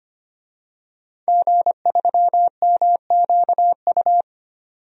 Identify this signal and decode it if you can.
Morse code, heard as a keyed tone, about 25 wpm, 700 Hz. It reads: G3MQU
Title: DXCC-100 Call Signs - 5 Characters - 25wpm